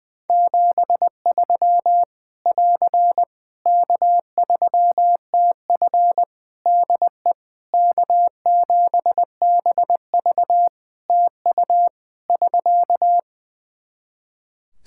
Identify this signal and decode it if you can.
Morse code, heard as a keyed tone, about 20 wpm, 700 Hz. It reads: 73 <AR> K3TF DE K7BV TU <SK>
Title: Practice QSO - Exchange 4 - Ending - 20wpm